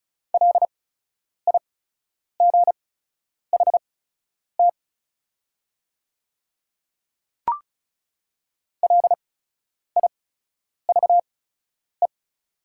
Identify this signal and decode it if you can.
Morse code, heard as a keyed tone, about 34 wpm, spaced as for 10 wpm, 700 Hz. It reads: LIGHT LIVE
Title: 8x Character Spacing - Top 300 Words - 35wpm